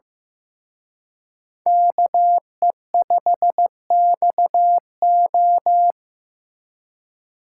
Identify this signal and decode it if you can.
Morse code, heard as a keyed tone, about 18 wpm, 700 Hz. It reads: KE5XO
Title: Speed Racing - USA 2x2 Call Signs 15wpm